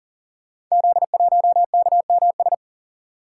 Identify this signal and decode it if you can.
Morse code, heard as a keyed tone, about 38 wpm, 690 Hz. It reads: Z1KMS
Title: Code Groups - 5 Letter:Numbers - 40wpm